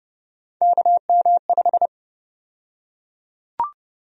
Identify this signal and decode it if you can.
Morse code, heard as a keyed tone, about 30 wpm, 700 Hz. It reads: KM5
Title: USA Call Sign Prefixes - 30wpm